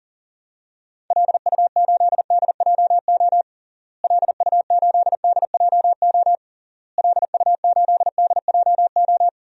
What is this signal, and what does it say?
Morse code, heard as a keyed tone, about 40 wpm, 700 Hz. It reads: LU8DJO LU8DJO LU8DJO
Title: DXCC-100 Call Signs - 6 Characters - Repeated 3x - 40wpm